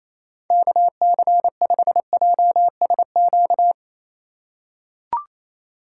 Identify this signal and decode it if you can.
Morse code, heard as a keyed tone, about 28 wpm, 700 Hz. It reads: KC5JSQ
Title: USA Call Signs - 3x Character Spacing - 28wpm